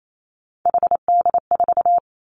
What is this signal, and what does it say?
Morse code, heard as a keyed tone, about 28 wpm, 700 Hz. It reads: HD4
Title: Three Challenging Letter-Number Combinations 28wpm